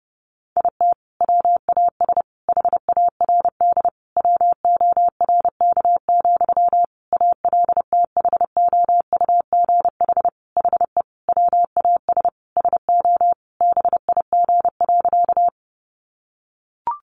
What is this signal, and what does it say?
Morse code, heard as a keyed tone, about 30 wpm, 700 Hz. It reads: IT WAS HARD WORK, ALTHOUGH HE WAS SO BIG.
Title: Sentences from Top 600 Words 30wpm